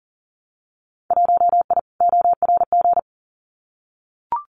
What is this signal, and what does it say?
Morse code, heard as a keyed tone, about 38 wpm, 700 Hz. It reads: 1I ORG
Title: Winter Field Day Exchanges Encore 40wpm